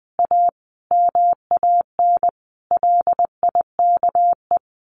Morse code, heard as a keyed tone, about 20 wpm, 700 Hz. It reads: A MAN LIKE
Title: Sentences from Top 1200 Words 20wpm